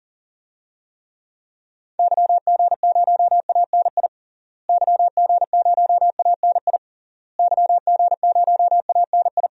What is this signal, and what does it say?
Morse code, heard as a keyed tone, about 40 wpm, 700 Hz. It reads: YG0ANI YG0ANI YG0ANI
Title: DXCC-100 Call Signs - 6 Characters - Repeated 3x - 40wpm